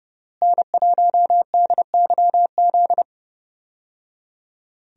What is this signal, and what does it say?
Morse code, heard as a keyed tone, about 30 wpm, 700 Hz. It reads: N1DYZ
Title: USA Call Signs - 5 Characters - Repeated 3x - 30wpm